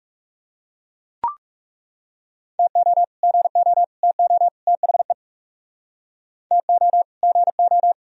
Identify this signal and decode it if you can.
Morse code, heard as a keyed tone, about 38 wpm, 700 Hz. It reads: TO GO TO THE TO GO